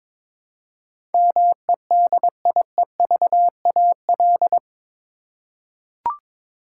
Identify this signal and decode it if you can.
Morse code, heard as a keyed tone, about 22 wpm, 700 Hz. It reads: MEDIEVAL
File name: practice / 22wpm / Eight Letter Words - 22wpm